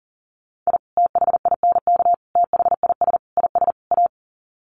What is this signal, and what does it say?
Morse code, heard as a keyed tone, about 40 wpm, 700 Hz. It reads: I THINK THIS IS A
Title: Sets of 5 Words 40wpm